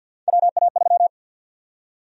Morse code, heard as a keyed tone, about 52 wpm, 700 Hz. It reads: WA3
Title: USA Call Sign Prefixes - 50wpm